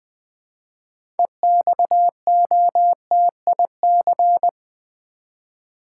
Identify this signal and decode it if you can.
Morse code, heard as a keyed tone, about 20 wpm, 700 Hz. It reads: EXOTIC
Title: 6 Letter Words - First 3 Letters Given - 20wpm